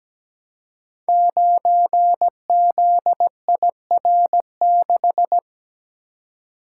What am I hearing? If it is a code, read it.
Morse code, heard as a keyed tone, about 17 wpm, 700 Hz. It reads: 9ZIR6
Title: Code Groups - 5 Letter:Numbers - 17wpm